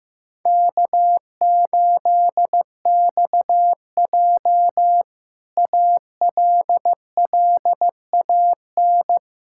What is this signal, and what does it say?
Morse code, heard as a keyed tone, about 15 wpm, 700 Hz. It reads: K8XJ ALLAN
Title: K1USN SST QSO Practice - 15wpm